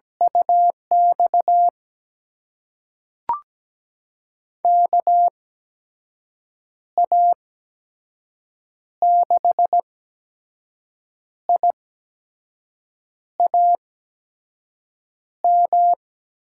Morse code, heard as a keyed tone, about 17 wpm, spaced as for 5 wpm, 700 Hz. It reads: UX KA6IAM
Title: USA Call Signs - 6 Characters - 8x Character Spacing - 17wpm